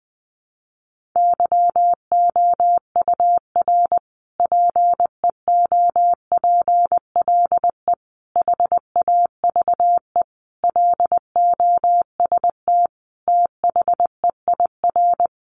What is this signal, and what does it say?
Morse code, heard as a keyed tone, about 20 wpm, 700 Hz. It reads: YOUR PEOPLE HAVE LOST THEIR
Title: Sentences from Top 1400 Words 20wpm